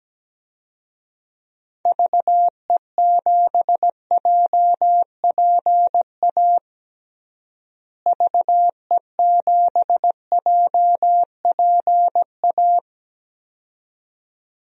Morse code, heard as a keyed tone, about 17 wpm, 700 Hz. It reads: VE7JPA VE7JPA
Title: DXCC-100 Call Signs - 6 Characters - Repeated 2x - 17wpm